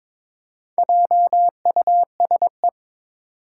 Morse code, heard as a keyed tone, about 22 wpm, 700 Hz. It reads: JUSE
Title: Four Challenging Letter-Number Combinations 22wpm